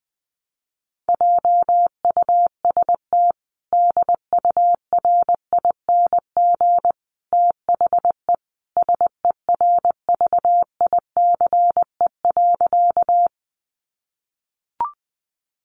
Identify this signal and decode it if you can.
Morse code, heard as a keyed tone, about 20 wpm, 700 Hz. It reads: JUST DURING THE SERVICE.
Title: Sentences from Top 700 Words 20wpm